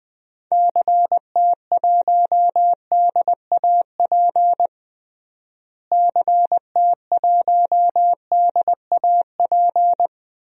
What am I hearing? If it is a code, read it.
Morse code, heard as a keyed tone, about 20 wpm, 700 Hz. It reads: CT1DAP CT1DAP
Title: DXCC-100 Call Signs - 6 Characters - Repeated 3x - 20wpm